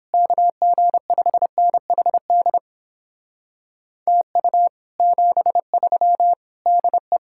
Morse code, heard as a keyed tone, about 26 wpm, 700 Hz. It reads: KG5NHD TU 73 DE
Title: Speed-Racing Common QSO Phrases Encore - 22wpm